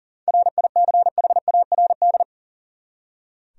Morse code, heard as a keyed tone, about 40 wpm, 700 Hz. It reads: RICHARD
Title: USA Common Names - 40wpm